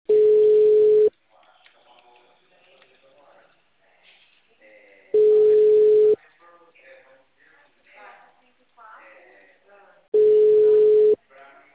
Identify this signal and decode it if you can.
Ringback tone (Europe, 425 Hz), with speech.